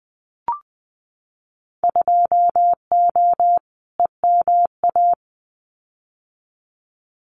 Morse code, heard as a keyed tone, about 20 wpm, 700 Hz. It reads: 2O EMA
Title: Winter Field Day Exchanges 20wpm